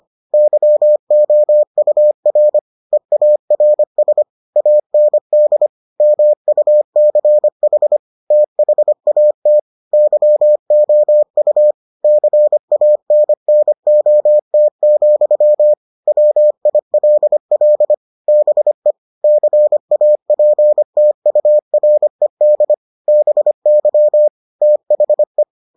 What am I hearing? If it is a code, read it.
Morse code, heard as a keyed tone, about 25 wpm, 600 Hz. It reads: YOUR EARS AND MUCH THAT YOU CANNOT, WILL BE CAPTURED BY THE